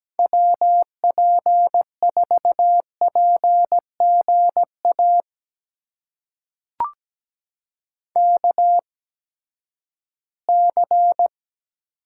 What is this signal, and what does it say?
Morse code, heard as a keyed tone, about 17 wpm, spaced as for 12 wpm, 700 Hz. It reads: WP4PGA KC
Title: USA Call Signs - 8x Character Spacing - 17wpm